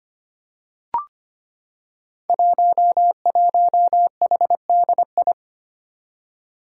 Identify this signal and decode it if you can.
Morse code, heard as a keyed tone, about 25 wpm, 700 Hz. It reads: 11HDI
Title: Five Challenging Letter-Number Combinations 25wpm